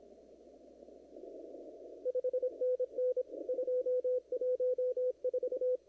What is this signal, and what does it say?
Morse code, heard as a keyed tone, about 25 wpm, 500 Hz. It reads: 5NN 214
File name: CW / kt5j wpx cw-op-k5pi